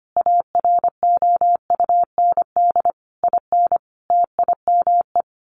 Morse code, heard as a keyed tone, about 25 wpm, 700 Hz. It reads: AROUND IN TIME
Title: A Sound Of Thunder 25wpm